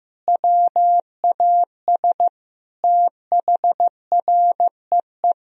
Morse code, heard as a keyed tone, about 16 wpm, 700 Hz. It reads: WAS THREE
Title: Sentences from Top 1000 Words - Review Part 3 - 15wpm